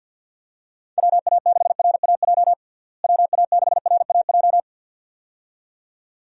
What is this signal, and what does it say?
Morse code, heard as a keyed tone, about 52 wpm, 700 Hz. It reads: WA6RAJ WA6RAJ
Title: USA Call Signs - 6 Characters - Repeated 2x - 50wpm